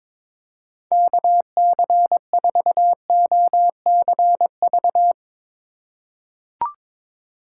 Morse code, heard as a keyed tone, about 22 wpm, 700 Hz. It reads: KC4OCV
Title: USA Call Signs - 6 Characters - 3x Character Spacing - 22wpm